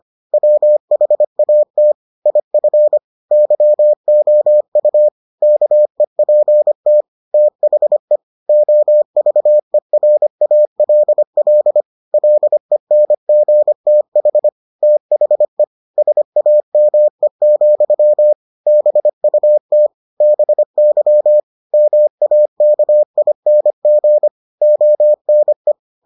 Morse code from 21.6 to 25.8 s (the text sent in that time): MAKING ONE